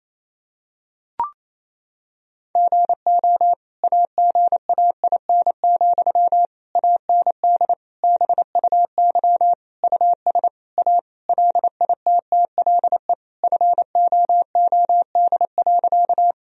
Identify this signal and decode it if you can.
Morse code, heard as a keyed tone, about 27 wpm, 700 Hz. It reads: GO AGAIN, AND BUY US A LITTLE FOOD.
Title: Sentences from Top 1000 Words - Review Part 4 - 28wpm